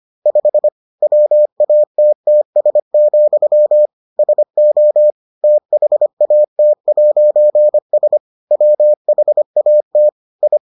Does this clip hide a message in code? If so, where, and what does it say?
Yes: Morse code from 0.3 to 10.6 s, heard as a keyed tone, about 25 wpm, 600 Hz. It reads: 5 WATTS, SO THAT'S WHAT I